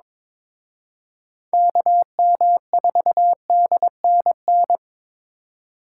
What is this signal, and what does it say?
Morse code, heard as a keyed tone, about 26 wpm, 700 Hz. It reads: KM4DNN
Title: Speed Racing - USA Call Signs 22wpm